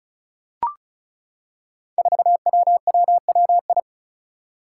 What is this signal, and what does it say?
Morse code, heard as a keyed tone, about 36 wpm, 700 Hz. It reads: 4WWWI
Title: Five Challenging Letter-Number Combinations 35wpm